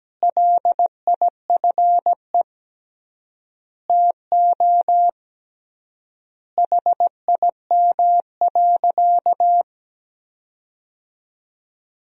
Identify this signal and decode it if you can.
Morse code, heard as a keyed tone, about 17 wpm, 700 Hz. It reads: LIFE TO HIM.
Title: 3x Word-Spacing - Sentences from Top 300 Words - 17wpm